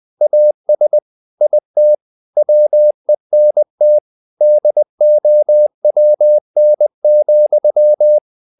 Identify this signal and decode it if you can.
Morse code, heard as a keyed tone, about 20 wpm, 600 Hz. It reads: AS IT WENT DOWN,
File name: the old man and the sea / the old man and the sea part 1 20wpm